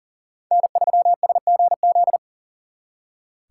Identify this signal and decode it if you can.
Morse code, heard as a keyed tone, about 50 wpm, 700 Hz. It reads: N3SGZ